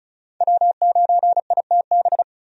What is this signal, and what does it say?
Morse code, heard as a keyed tone, about 36 wpm, 700 Hz. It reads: W9ITB
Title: USA Call Signs - 5 Characters - 35wpm